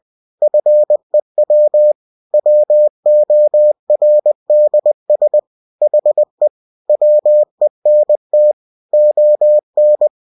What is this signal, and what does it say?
Morse code, heard as a keyed tone, about 20 wpm, 600 Hz. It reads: FEW WORDS HE WENT ON
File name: cw-books / winnie-the-pooh / winnie-20wpm-page-0022